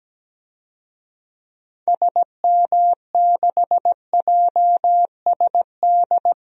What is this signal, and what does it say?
Morse code, heard as a keyed tone, about 17 wpm, 700 Hz. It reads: SM6JSD
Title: DXCC-100 Call Signs - 6 Characters - 17wpm